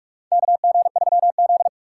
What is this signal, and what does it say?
Morse code, heard as a keyed tone, about 46 wpm, 700 Hz. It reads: KG3Z
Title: USA Call Signs - 2x Character Spacing - 45wpm